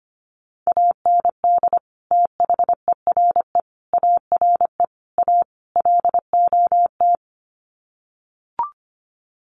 Morse code, heard as a keyed tone, about 25 wpm, 700 Hz. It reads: AND THERE ARE A LOT
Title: Sets of 5 Words Encore2 25wpm